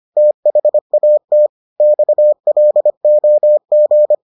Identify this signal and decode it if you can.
Morse code, heard as a keyed tone, about 25 wpm, 600 Hz. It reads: THAT XLOG